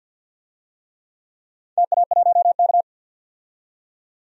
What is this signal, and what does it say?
Morse code, heard as a keyed tone, about 52 wpm, 700 Hz. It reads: TA1K
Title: DXCC-100 Call Signs - 4 Characters - 50wpm